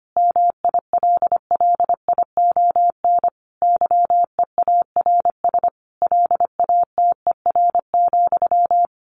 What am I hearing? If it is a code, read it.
Morse code, heard as a keyed tone, about 25 wpm, 700 Hz. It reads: MILLION YEARS LATER,
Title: A Sound Of Thunder 25wpm